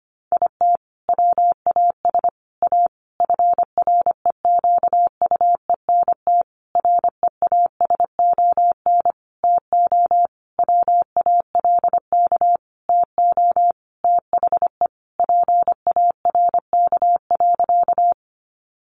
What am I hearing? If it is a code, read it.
Morse code, heard as a keyed tone, about 25 wpm, 700 Hz. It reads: IT WAS A FREQUENT REASON TO WALK TO THE PARK.
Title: Sentences from Top 1400 Words 25wpm